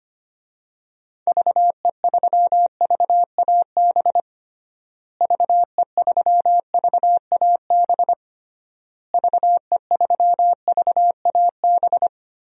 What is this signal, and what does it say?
Morse code, heard as a keyed tone, about 25 wpm, 700 Hz. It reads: VE3VAB VE3VAB VE3VAB
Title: DXCC-100 Call Signs - 6 Characters - Repeated 3x - 25wpm